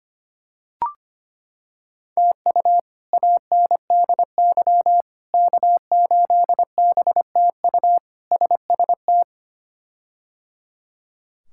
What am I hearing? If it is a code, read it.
Morse code, heard as a keyed tone, about 25 wpm, 700 Hz. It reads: TU ANDY K8BTU SST
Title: K1USN SST QSO Practice - 25wpm